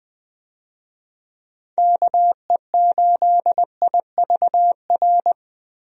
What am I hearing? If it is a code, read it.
Morse code, heard as a keyed tone, about 20 wpm, 700 Hz. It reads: KE8IVR
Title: USA Call Signs - 6 Characters - Repeated 2x - 20wpm